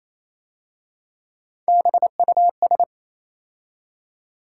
Morse code, heard as a keyed tone, about 27 wpm, 700 Hz. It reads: BUS
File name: Easy Words - Speed-Racing with Hints - 22wpm